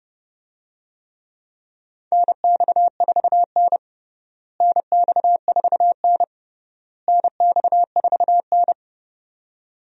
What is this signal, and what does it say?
Morse code, heard as a keyed tone, about 30 wpm, 700 Hz. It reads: NX4N NX4N NX4N
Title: USA Call Signs - 4 Characters - Repeated 3x - 30wpm